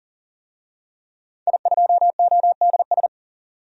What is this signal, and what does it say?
Morse code, heard as a keyed tone, about 40 wpm, 700 Hz. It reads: I2ODS
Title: DXCC-100 Call Signs - 5 Characters - 40wpm